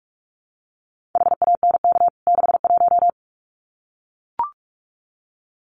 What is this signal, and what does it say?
Morse code, heard as a keyed tone, about 44 wpm, 700 Hz. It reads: HANK 61
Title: CWT Exchange 45wpm